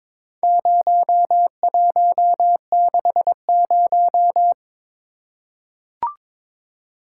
Morse code, heard as a keyed tone, about 22 wpm, 700 Hz. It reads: 0160
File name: practice / 22wpm / Four Letter-Number Combinations - 22wpm